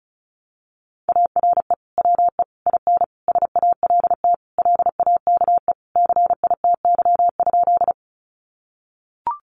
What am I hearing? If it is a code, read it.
Morse code, heard as a keyed tone, about 34 wpm, 700 Hz. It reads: ARE WE IN SALT LAKE CITY?
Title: Sentences from Top 1100 Words 35wpm